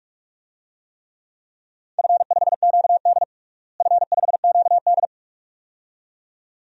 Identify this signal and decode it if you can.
Morse code, heard as a keyed tone, about 41 wpm, 700 Hz. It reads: F5QD F5QD
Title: DXCC-100 Call Signs - 4 Characters - Repeated 2x - 45wpm